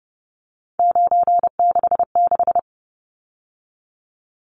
Morse code, heard as a keyed tone, about 30 wpm, 700 Hz. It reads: 966
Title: Numbers - 3-Digit 30wpm